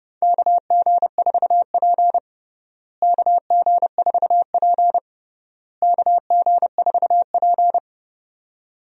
Morse code, heard as a keyed tone, about 30 wpm, 700 Hz. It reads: KG4P KG4P KG4P
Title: USA Call Signs - 4 Characters - Repeated 3x - 30wpm